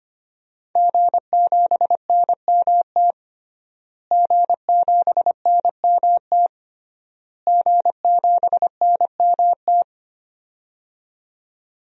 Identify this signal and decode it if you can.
Morse code, heard as a keyed tone, about 25 wpm, 700 Hz. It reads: G7NMT G7NMT G7NMT
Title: DXCC-100 Call Signs - 5 Characters - Repeated 3x - 25wpm